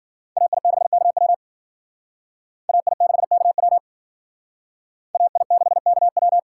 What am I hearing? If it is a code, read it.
Morse code, heard as a keyed tone, about 51 wpm, 700 Hz. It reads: AI6KW AI6KW AI6KW